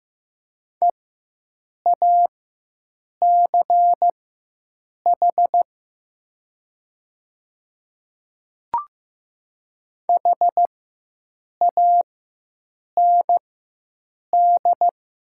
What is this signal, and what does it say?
Morse code, heard as a keyed tone, about 15 wpm, spaced as for 7 wpm, 700 Hz. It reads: EACH HAND
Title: 4x Character Spacing - Top 200 Words - 15wpm